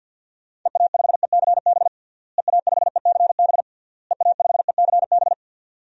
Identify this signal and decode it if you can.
Morse code, heard as a keyed tone, about 50 wpm, 700 Hz. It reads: EA5ECB EA5ECB EA5ECB
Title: DXCC-100 Call Signs - 6 Characters - Repeated 3x - 50wpm